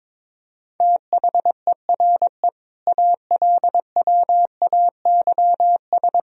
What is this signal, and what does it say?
Morse code, heard as a keyed tone, about 22 wpm, 700 Hz. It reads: THERE ALWAYS
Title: Sentences from Top 1000 Words - Review Part 2 - 22wpm